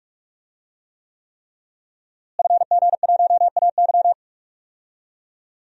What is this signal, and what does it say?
Morse code, heard as a keyed tone, about 42 wpm, 700 Hz. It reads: FG1AY
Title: DXCC-100 Call Signs - 5 Characters - Repeated 3x - 45wpm